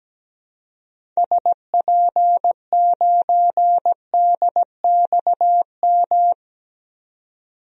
Morse code, heard as a keyed tone, about 17 wpm, 700 Hz. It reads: SP9DXM
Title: DXCC-100 Call Signs - 6 Characters - 17wpm